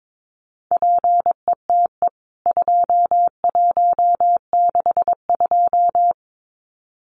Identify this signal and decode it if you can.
Morse code, heard as a keyed tone, about 22 wpm, 700 Hz. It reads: PETE 2162
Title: CWT Exchange 22wpm